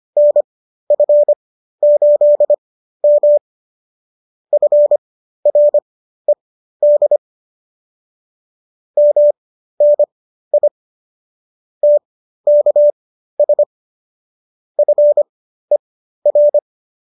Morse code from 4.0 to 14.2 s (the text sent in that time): FRED MNI TKS